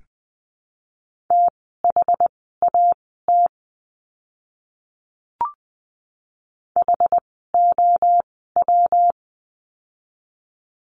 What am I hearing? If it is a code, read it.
Morse code, heard as a keyed tone, about 20 wpm, spaced as for 14 wpm, 700 Hz. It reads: THAT HOW
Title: 115 Lesson 19f - Letter w - Words 20wpm